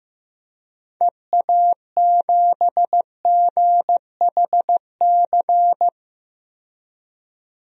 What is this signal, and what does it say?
Morse code, heard as a keyed tone, about 15 wpm, 700 Hz. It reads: EA7GHC
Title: DXCC-100 Call Signs - 6 Characters - 15wpm